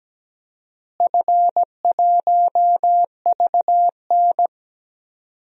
Morse code, heard as a keyed tone, about 17 wpm, 700 Hz. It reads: F1VN
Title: DXCC-100 Call Signs - 4 Characters - Repeated 3x - 17wpm